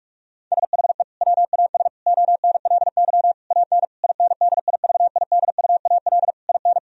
Morse code, heard as a keyed tone, about 46 wpm, 700 Hz. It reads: SHE WAS ONLY AN INDIVIDUAL IN
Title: Sentences from Top 1000 Words - Review Part 4 - 45wpm